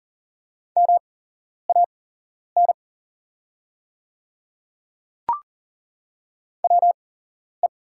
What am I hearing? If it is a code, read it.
Morse code, heard as a keyed tone, about 40 wpm, spaced as for 11 wpm, 700 Hz. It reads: MAN WE
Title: 8x Character Spacing - Top 200 Words - 40wpm